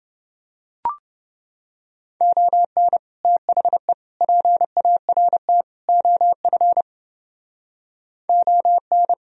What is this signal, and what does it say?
Morse code, heard as a keyed tone, about 29 wpm, 700 Hz. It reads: ON THE PART OF ON